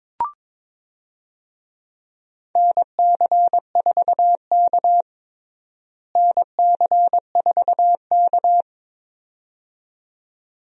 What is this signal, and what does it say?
Morse code, heard as a keyed tone, about 22 wpm, 700 Hz. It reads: NC4K NC4K
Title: USA Call Signs - 4 Characters - Repeated 2x - 22wpm